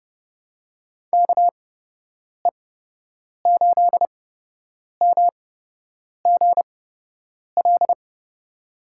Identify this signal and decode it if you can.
Morse code, heard as a keyed tone, about 30 wpm, spaced as for 8 wpm, 700 Hz. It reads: KE8MGL
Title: USA Call Signs - 8x Character Spacing - 30wpm